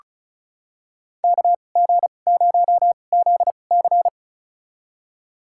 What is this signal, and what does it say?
Morse code, heard as a keyed tone, about 34 wpm, spaced as for 25 wpm, 700 Hz. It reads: KG0ZC
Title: USA Call Signs - 5 Characters - 2x Character Spacing - 35wpm